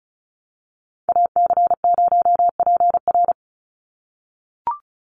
Morse code, heard as a keyed tone, about 36 wpm, 700 Hz. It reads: AC0PR
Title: NA SOTA Activator Callsigns 35wpm